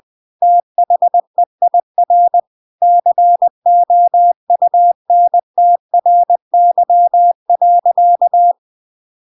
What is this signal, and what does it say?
Morse code, heard as a keyed tone, about 20 wpm, 700 Hz. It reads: THEIR COUNTRY.
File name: CW Learning Programs / sounds and personalities of cw and transmitters / 14 - Now20